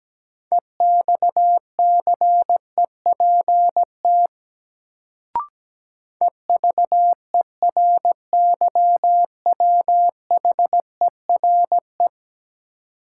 Morse code, heard as a keyed tone, about 17 wpm, 700 Hz. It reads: EXCEPT EVERYWHERE
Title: Sentences from Top 1600 Words - New Word Drill - 17wpm